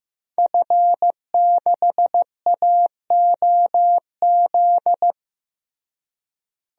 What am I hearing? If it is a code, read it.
Morse code, heard as a keyed tone, about 15 wpm, 700 Hz. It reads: F6AOZ
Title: DXCC-100 Call Signs - 5 Characters - Repeated 3x - 15wpm